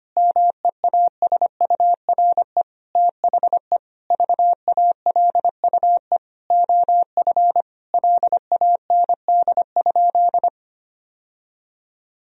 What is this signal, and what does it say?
Morse code, heard as a keyed tone, about 25 wpm, 700 Hz. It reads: MEASURE THE VALUE OF LAND?
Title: Sentences from Top 1000 Words - Review Part 3 - 25wpm